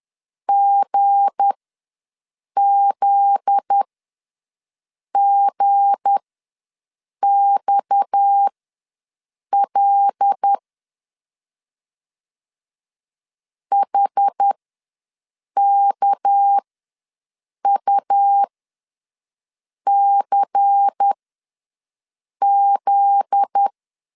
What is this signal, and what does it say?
Morse code, heard as a keyed tone, about 10 wpm, spaced as for 5 wpm, 800 Hz. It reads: GZGXL HKUCZ
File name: LEZIONI / 7 GF9Z / 7 GF9Z ws 10wpm 3919